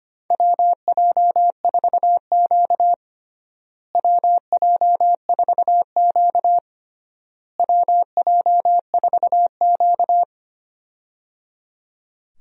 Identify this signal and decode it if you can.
Morse code, heard as a keyed tone, about 25 wpm, 700 Hz. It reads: WJ4Q WJ4Q WJ4Q
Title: USA Call Signs - 4 Characters - Repeated 3x - 25wpm